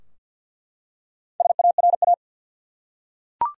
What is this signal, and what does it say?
Morse code, heard as a keyed tone, about 48 wpm, 700 Hz. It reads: SARA
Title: USA Common Names - Max 4 Letters - 50wpm